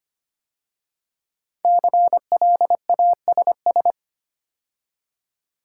Morse code, heard as a keyed tone, about 25 wpm, 700 Hz. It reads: CLASS
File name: 5 Letter Words - First 3 Letters Given - 25wpm